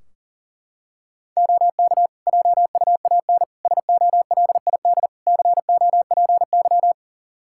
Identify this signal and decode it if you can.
Morse code, heard as a keyed tone, about 40 wpm, 700 Hz. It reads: OK JUAN SOLID COPY
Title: K1USN SST QSO Practice 40wpm